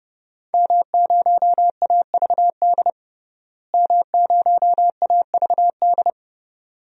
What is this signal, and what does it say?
Morse code, heard as a keyed tone, about 31 wpm, 700 Hz. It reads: M0AVD M0AVD
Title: DXCC-100 Call Signs - 5 Characters - Repeated 2x - 30wpm